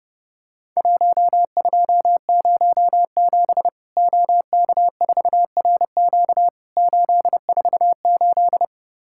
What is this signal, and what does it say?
Morse code, heard as a keyed tone, about 30 wpm, 700 Hz. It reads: 1207 OK4RQ 848
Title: Worked All Europe DX Contest - QTC Exchange - 30wpm